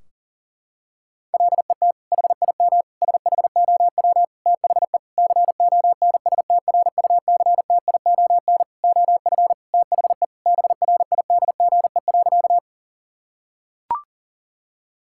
Morse code, heard as a keyed tone, about 40 wpm, 700 Hz. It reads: LET HIM SHOW THE CONSTRUCTION OF THE BRIDGE.
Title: Sentences from Top 1600 Words 40wpm